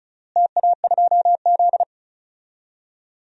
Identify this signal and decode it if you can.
Morse code, heard as a keyed tone, about 34 wpm, 700 Hz. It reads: TA2Z
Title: DXCC-100 Call Signs - 4 Characters - Repeated 3x - 35wpm